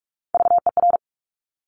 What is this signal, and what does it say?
Morse code, heard as a keyed tone, about 46 wpm, 700 Hz. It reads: VER
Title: Top Three Letter Combinations in Words 45wpm